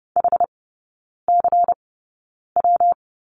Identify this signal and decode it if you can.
Morse code, heard as a keyed tone, about 30 wpm, spaced as for 9 wpm, 700 Hz. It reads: HCW
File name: Lesson 19e - Letter w - Three Characters 30wpm